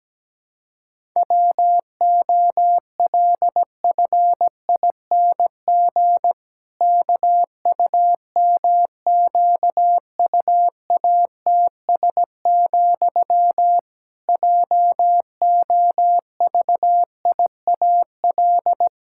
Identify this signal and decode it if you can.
Morse code, heard as a keyed tone, about 17 wpm, 700 Hz. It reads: WOLFING KUMQUATS, JOVIAL
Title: Pangram Challenge v2 - 17wpm